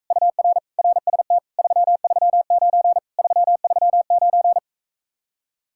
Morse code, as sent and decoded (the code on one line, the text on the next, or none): ..- .-. / .-. ... - / ...-- ...-- ----. / ...-- ...-- ----.
UR RST 339 339